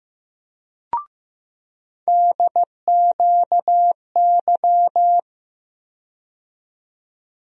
Morse code, heard as a keyed tone, about 15 wpm, 700 Hz. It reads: DQY
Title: USA Call Sign Postfixes - 3 Letters 15wpm